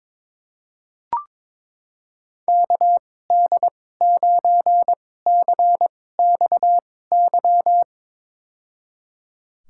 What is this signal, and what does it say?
Morse code, heard as a keyed tone, about 22 wpm, spaced as for 16 wpm, 700 Hz. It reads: KD9CXY
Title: USA Call Signs - 2x Character Spacing - 22wpm